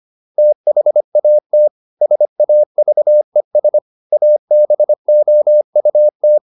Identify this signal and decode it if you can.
Morse code, heard as a keyed tone, about 25 wpm, 600 Hz. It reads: THAT SAVES ABOUT